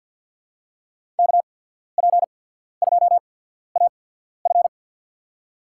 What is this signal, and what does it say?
Morse code, heard as a keyed tone, about 47 wpm, spaced as for 14 wpm, 700 Hz. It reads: KP2AF
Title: USA Call Signs - 5 Characters - 8x Character Spacing - 50wpm